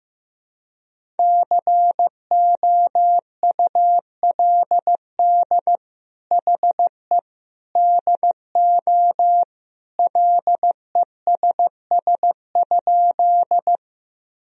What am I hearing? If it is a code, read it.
Morse code, heard as a keyed tone, about 16 wpm, 700 Hz. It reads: COULD HE DO LESS?
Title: Sentences from Top 1000 Words - Review Part 2 - 15wpm